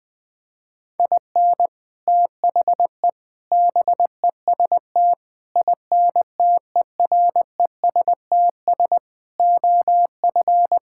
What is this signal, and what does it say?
Morse code, heard as a keyed tone, about 20 wpm, 700 Hz. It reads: IN THE BEST INTERESTS OF
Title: Sets of 6 Words - 20wpm